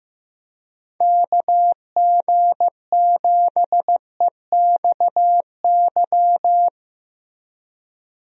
Morse code, heard as a keyed tone, about 15 wpm, 700 Hz. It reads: KG7EXY
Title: USA Call Signs - 6 Characters - Repeated 2x - 15wpm